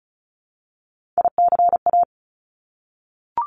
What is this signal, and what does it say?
Morse code, heard as a keyed tone, about 36 wpm, 700 Hz. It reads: ICA
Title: Top Three Letter Combinations in Words 35wpm